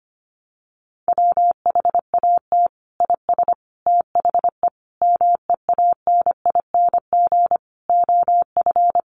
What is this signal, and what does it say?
Morse code, heard as a keyed tone, about 25 wpm, 700 Hz. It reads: WHAT IS THE MEANING OF
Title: Sentences from Top 900 Words 25wpm